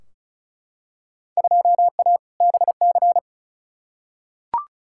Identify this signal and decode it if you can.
Morse code, heard as a keyed tone, about 34 wpm, 700 Hz. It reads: 2A BC
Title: Field Day Exchanges - 35wpm